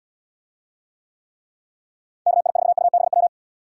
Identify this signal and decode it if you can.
Morse code, heard as a keyed tone, about 77 wpm, 700 Hz. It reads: KI4UBW